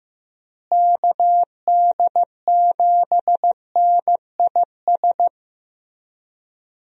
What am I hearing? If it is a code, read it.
Morse code, heard as a keyed tone, about 15 wpm, 700 Hz. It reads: KD7NIS
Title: USA Call Signs - 6 Characters - Repeated 2x - 15wpm